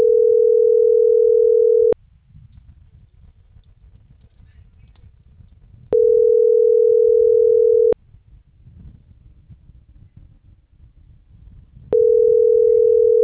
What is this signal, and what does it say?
Ringback tone (North America), 440 + 480 Hz.